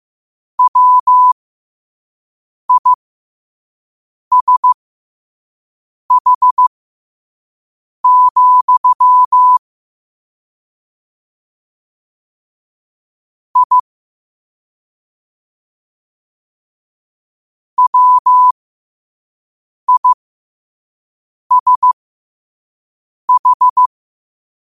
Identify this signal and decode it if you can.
Morse code, heard as a keyed tone, about 15 wpm, spaced as for 5 wpm, 1000 Hz. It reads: WISH, I WISH
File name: qotdfiles / 5wpm